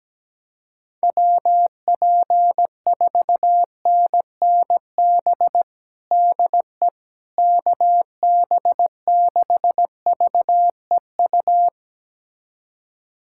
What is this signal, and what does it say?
Morse code, heard as a keyed tone, about 17 wpm, 700 Hz. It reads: WP4NNB DE KB6VEU
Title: USA Callsign DE Callsign - 6 Characters - 17wpm